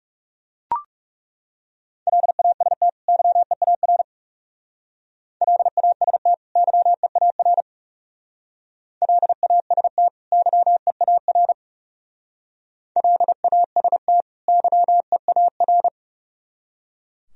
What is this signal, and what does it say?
Morse code, heard as a keyed tone, about 36 wpm, 700 Hz. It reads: LAST YEAR LAST YEAR LAST YEAR LAST YEAR
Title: Speed-Racing - Sets of 2 Words 30wpm